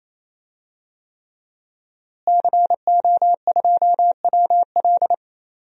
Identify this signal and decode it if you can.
Morse code, heard as a keyed tone, about 27 wpm, 700 Hz. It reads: CO2WL
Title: DXCC-100 Call Signs - 5 Characters - Repeated 3x - 28wpm